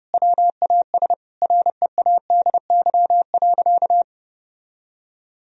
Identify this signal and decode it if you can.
Morse code, heard as a keyed tone, about 30 wpm, 700 Hz. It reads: WAS READY.
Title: Sentences from Top 1000 Words - Review Part 3 - 30wpm